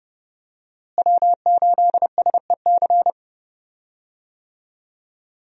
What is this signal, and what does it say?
Morse code, heard as a keyed tone, about 30 wpm, 700 Hz. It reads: W8SEC
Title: USA Call Signs - 5 Characters - Repeated 2x - 30wpm